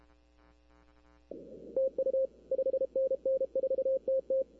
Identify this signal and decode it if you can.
Morse code, heard as a keyed tone, about 32 wpm, 530 Hz. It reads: TU 5NN 4TT
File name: arrl cw dx